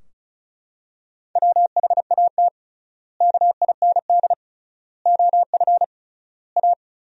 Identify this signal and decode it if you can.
Morse code, heard as a keyed tone, about 34 wpm, 700 Hz. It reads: WHAT KIND OF A
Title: 3x Word-Spacing - Sentences from Top 400 Words - 35wpm